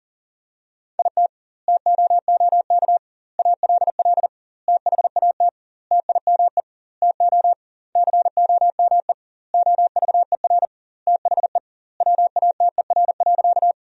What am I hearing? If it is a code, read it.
Morse code, heard as a keyed tone, about 40 wpm, 700 Hz. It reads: IT TOOK ALL THAT TIME TO COME OVER THE WATER.
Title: 2x Word-Spacing - Sentences from Top 300 Words - 40wpm